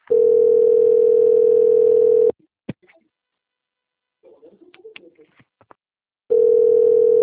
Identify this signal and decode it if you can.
Ringback tone (North America), 440 + 480 Hz.
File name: out- - - - - .11968